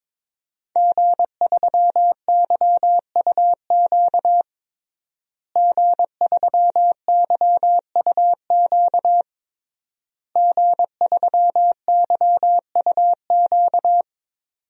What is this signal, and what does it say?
Morse code, heard as a keyed tone, about 22 wpm, 700 Hz. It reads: G3YUQ G3YUQ G3YUQ
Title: DXCC-100 Call Signs - 5 Characters - Repeated 3x - 22wpm